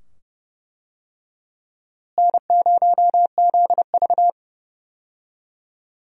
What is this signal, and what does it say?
Morse code, heard as a keyed tone, about 29 wpm, 700 Hz. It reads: N0ZV
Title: USA Call Signs - 4 Characters - Repeated 2x - 30wpm